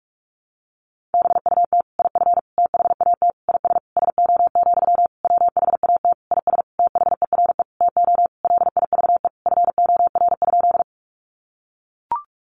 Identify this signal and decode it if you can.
Morse code, heard as a keyed tone, about 44 wpm, 700 Hz. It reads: BUT IF THAT IS SO, WHAT IS THERE TO LIVE FOR?
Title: Sentences from Top 200 Words 45wpm